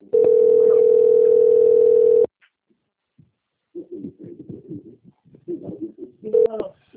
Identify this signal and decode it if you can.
Ringback tone (North America), 440 + 480 Hz.